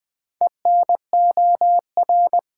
Morse code, heard as a keyed tone, about 20 wpm, 700 Hz. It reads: ENOR
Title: Build A Word - 4 Letters At A Time - 8-Letter Words - 20wpm